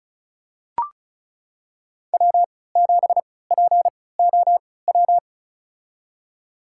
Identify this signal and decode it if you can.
Morse code, heard as a keyed tone, about 36 wpm, spaced as for 20 wpm, 700 Hz. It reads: W7POW
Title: USA Call Signs - 5 Characters - 3x Character Spacing - 35wpm